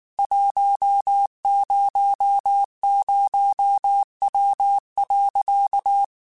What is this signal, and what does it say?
Morse code, heard as a keyed tone, about 19 wpm, 800 Hz. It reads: 100W.
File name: QSOs / QSOs-mp3s / WD0FBT 1@18